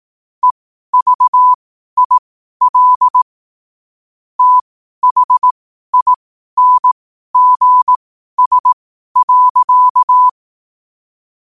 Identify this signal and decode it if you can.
Morse code, heard as a keyed tone, about 18 wpm, spaced as for 12 wpm, 1000 Hz. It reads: EVIL THINGS.
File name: qotdfiles / 12wpm